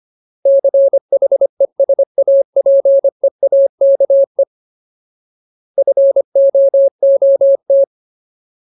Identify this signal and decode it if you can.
Morse code, heard as a keyed tone, about 25 wpm, 550 Hz. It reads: CHESAPEAKE FOOT
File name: Ultimate Word List / 25WPM Ultimate